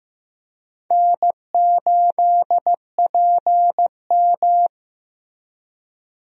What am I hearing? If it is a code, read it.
Morse code, heard as a keyed tone, about 15 wpm, 700 Hz. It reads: N8PM
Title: USA Call Signs - 4 Characters - Repeated 2x - 15wpm